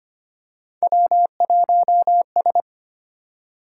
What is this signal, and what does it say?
Morse code, heard as a keyed tone, about 25 wpm, 700 Hz. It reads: W1S
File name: Three Letter-Number Combinations - 25wpm